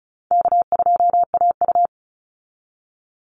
Morse code, heard as a keyed tone, about 34 wpm, 700 Hz. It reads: K2AU
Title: Call Signs Repeated 3x - 35wpm